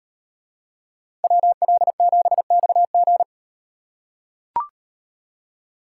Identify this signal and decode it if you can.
Morse code, heard as a keyed tone, about 38 wpm, 700 Hz. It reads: WL7XG